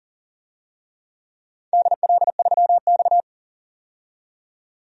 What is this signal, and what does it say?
Morse code, heard as a keyed tone, about 40 wpm, 700 Hz. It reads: DL3X
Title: DXCC-100 Call Signs - 4 Characters - 40wpm